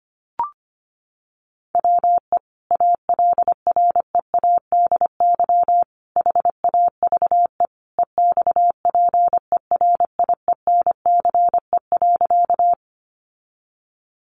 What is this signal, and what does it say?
Morse code, heard as a keyed tone, about 25 wpm, 700 Hz. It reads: WE ALREADY HAVE EXPERIENCE.
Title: Sentences from Top 700 Words 25wpm